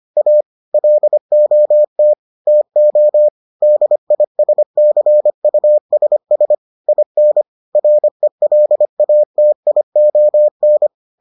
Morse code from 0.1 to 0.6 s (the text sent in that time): A